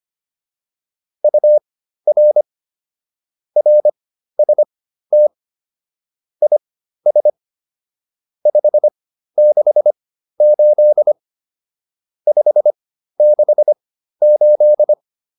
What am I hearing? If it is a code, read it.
Morse code, heard as a keyed tone, about 25 wpm, spaced as for 13 wpm, 600 Hz. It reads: UR RST IS 568 568
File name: QSO310 13